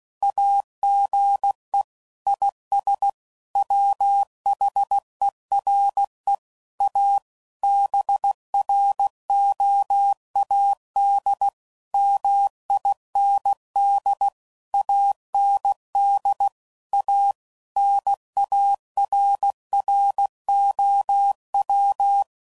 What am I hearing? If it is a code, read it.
Morse code, heard as a keyed tone, about 16 wpm, 800 Hz. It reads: AGE IS WHERE A BROAD MIND AND A NARROW